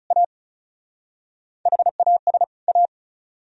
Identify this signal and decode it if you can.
Morse code, heard as a keyed tone, about 44 wpm, 700 Hz. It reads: A HAS A